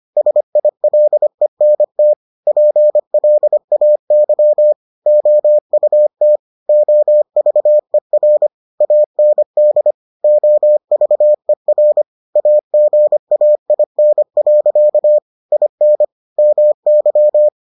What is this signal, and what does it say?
Morse code, heard as a keyed tone, about 25 wpm, 600 Hz. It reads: SILENT PLAY OUT OVER AND OVER AGAIN. IN MY